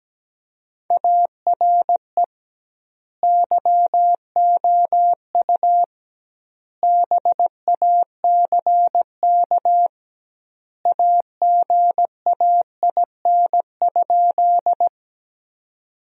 Morse code, heard as a keyed tone, about 17 wpm, 700 Hz. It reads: ARE YOU BACK AGAIN?
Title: 2x Word-Spacing - Sentences from Top 200 Words - 17wpm